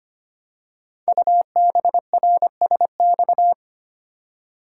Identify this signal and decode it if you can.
Morse code, heard as a keyed tone, about 25 wpm, 700 Hz. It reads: UBRSX
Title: Code Groups - 5 Letters - 25wpm